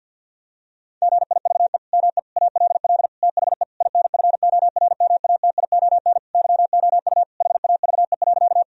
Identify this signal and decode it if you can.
Morse code, heard as a keyed tone, about 52 wpm, 700 Hz. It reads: GIVE ME ALL THE INFORMATION YOU HAVE.
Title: Sentences from Top 1000 Words - Review Part 3 - 50wpm